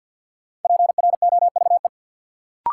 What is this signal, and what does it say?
Morse code, heard as a keyed tone, about 53 wpm, 700 Hz. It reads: PROVE
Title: Sentences from Top 1600 Words - New Word Drill Alt - 50wpm